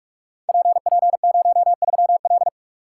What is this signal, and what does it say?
Morse code, heard as a keyed tone, about 44 wpm, 700 Hz. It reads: PP03L
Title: Code Groups - 5 Letter:Numbers - 45wpm